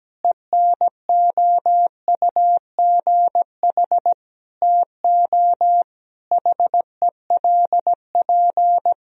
Morse code, heard as a keyed tone, about 17 wpm, 700 Hz. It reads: ENOUGH TO HELP
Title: Sentences from Top 1000 Words - Review Part 3 - 17wpm